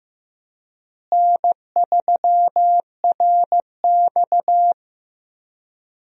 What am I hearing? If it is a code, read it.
Morse code, heard as a keyed tone, about 15 wpm, 700 Hz. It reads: N3RX
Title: USA Call Signs - 4 Characters - Repeated 3x - 15wpm